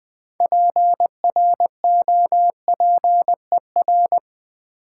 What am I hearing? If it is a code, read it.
Morse code, heard as a keyed tone, about 21 wpm, 700 Hz. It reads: PROPER
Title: Sentences from Top 800 Words - New Word Drill - 20wpm